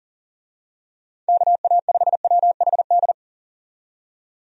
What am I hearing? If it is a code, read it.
Morse code, heard as a keyed tone, about 40 wpm, 700 Hz. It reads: KA5WHD
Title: USA Call Signs - 6 Characters - 40wpm